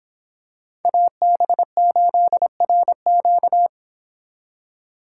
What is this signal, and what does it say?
Morse code, heard as a keyed tone, about 26 wpm, 700 Hz. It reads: AB8RQ